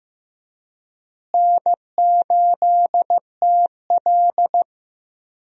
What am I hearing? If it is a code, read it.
Morse code, heard as a keyed tone, about 15 wpm, 700 Hz. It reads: N8TL
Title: USA Call Signs - 4 Characters - Repeated 3x - 15wpm